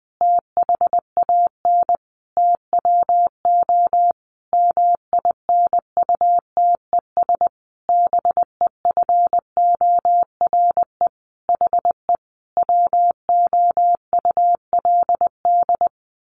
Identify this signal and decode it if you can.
Morse code, heard as a keyed tone, about 20 wpm, 700 Hz. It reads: THAN TWO MINUTES BEFORE HE WOULD
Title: A Sound Of Thunder 20wpm